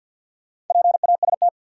Morse code, heard as a keyed tone, about 47 wpm, 700 Hz. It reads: PAST
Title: Sets of 2 Words - New Word Drill - 50wpm